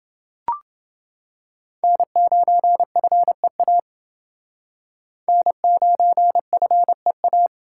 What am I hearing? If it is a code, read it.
Morse code, heard as a keyed tone, about 28 wpm, 700 Hz. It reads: N9FEA N9FEA